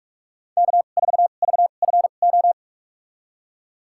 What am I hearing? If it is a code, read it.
Morse code, heard as a keyed tone, about 41 wpm, spaced as for 31 wpm, 700 Hz. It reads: K4VFO
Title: USA Call Signs - 5 Characters - 2x Character Spacing - 45wpm